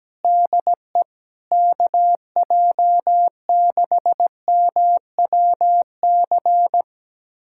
Morse code, heard as a keyed tone, about 17 wpm, 700 Hz. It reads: DE KJ6MWC
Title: USA Callsign DE Callsign 17wpm